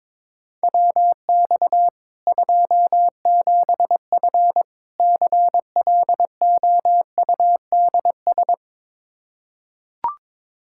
Morse code, heard as a keyed tone, about 22 wpm, 700 Hz. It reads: WX 27F CLOUDS
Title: K1USN SST QSO Practice 22wpm